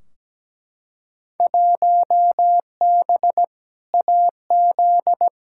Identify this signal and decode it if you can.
Morse code, heard as a keyed tone, about 17 wpm, 700 Hz. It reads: 1B AZ
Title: Field Day Exchanges Encore - 17wpm